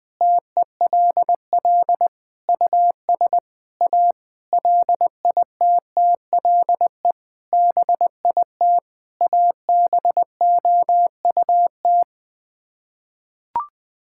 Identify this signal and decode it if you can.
Morse code, heard as a keyed tone, about 20 wpm, 700 Hz. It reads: TELL US A LITTLE BIT ABOUT
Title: Sets of 8 Words - 20wpm